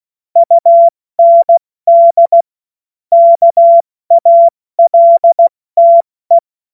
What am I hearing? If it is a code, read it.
Morse code, heard as a keyed tone, about 16 wpm, 680 Hz. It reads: UND KALTE
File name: Morsekurs / Witze / Witz 09. Dezember 2025 70 F80